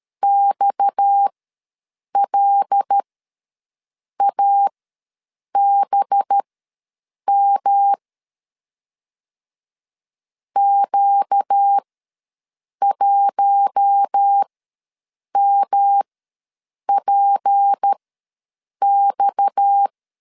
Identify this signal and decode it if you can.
Morse code, heard as a keyed tone, about 13 wpm, spaced as for 7 wpm, 800 Hz. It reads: XLABM Q1MPX
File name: LEZIONI / 6 PKX0 / 6 PKX ws 12wpm 3106